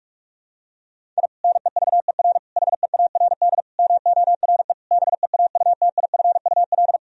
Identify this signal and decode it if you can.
Morse code, heard as a keyed tone, about 42 wpm, 700 Hz. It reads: I NEVER HEARD MORE BEAUTIFUL
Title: Sentences from Top 1000 Words - Review Part 1 - 45wpm